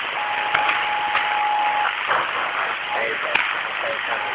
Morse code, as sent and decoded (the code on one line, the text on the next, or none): none